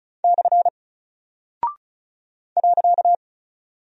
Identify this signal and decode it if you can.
Morse code, heard as a keyed tone, about 36 wpm, 700 Hz. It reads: / .
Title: Single Character (Letter:Number:Punctuation:Prosign) Alt 35wpm